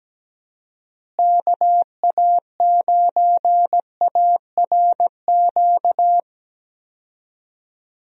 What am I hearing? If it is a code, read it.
Morse code, heard as a keyed tone, about 17 wpm, 700 Hz. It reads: KA9ARQ
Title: USA Call Signs - 6 Characters - Repeated 2x - 17wpm